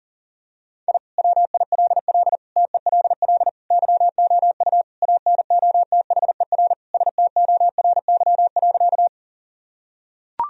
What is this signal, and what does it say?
Morse code, heard as a keyed tone, about 40 wpm, 700 Hz. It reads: I WILL TELL YOU ANOTHER STORY.
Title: Sentences from Top 1000 Words - Review Part 1 - 40wpm